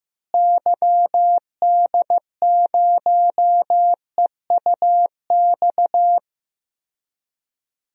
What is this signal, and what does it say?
Morse code, heard as a keyed tone, about 15 wpm, 700 Hz. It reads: YD0EUX
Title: DXCC-100 Call Signs - 6 Characters - Repeated 3x - 15wpm